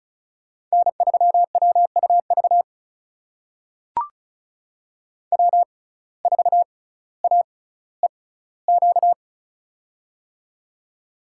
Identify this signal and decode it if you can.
Morse code, heard as a keyed tone, about 34 wpm, spaced as for 18 wpm, 700 Hz. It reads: N3WUV W4AEQ
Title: USA Call Signs - 5 Characters - 6x Character Spacing - 35wpm